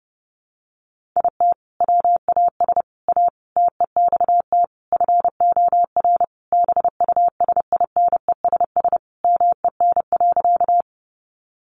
Morse code, heard as a keyed tone, about 28 wpm, 700 Hz. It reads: IT WAS A TEXT FOR BUSINESS MEN.
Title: Sentences from Top 1200 Words 30wpm